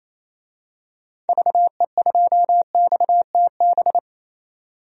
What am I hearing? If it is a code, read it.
Morse code, heard as a keyed tone, about 28 wpm, 700 Hz. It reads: VE2XTB
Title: DXCC-100 Call Signs - 6 Characters - 28wpm